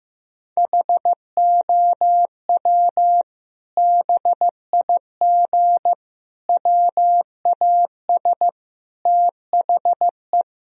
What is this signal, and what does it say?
Morse code, heard as a keyed tone, about 16 wpm, 700 Hz. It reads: HOW BIG WAS THE
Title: Sentences from Top 1000 Words - Review Part 4 - 15wpm